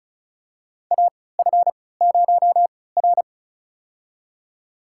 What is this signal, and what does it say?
Morse code, heard as a keyed tone, about 34 wpm, spaced as for 20 wpm, 700 Hz. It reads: AF0R
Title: USA Call Signs - 3x Character Spacing - 35wpm